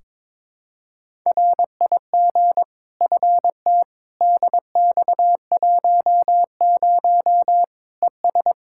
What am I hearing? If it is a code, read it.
Morse code, heard as a keyed tone, about 21 wpm, 700 Hz. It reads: RIG FT DX10 ES
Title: Practice QSO - Exchange 2 - Info - 22wpm